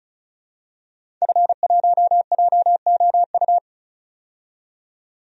Morse code, heard as a keyed tone, about 35 wpm, 700 Hz. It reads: F1JOU
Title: DXCC-100 Call Signs - 5 Characters - 35wpm